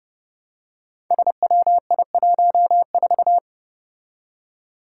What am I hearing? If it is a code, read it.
Morse code, heard as a keyed tone, about 29 wpm, 700 Hz. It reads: SWI14
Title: Five Challenging Letter-Number Combinations 30wpm